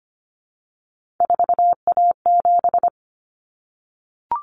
Morse code, heard as a keyed tone, about 25 wpm, 700 Hz. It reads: 4A7
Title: Common Cut Numbers - 3-Digit 25wpm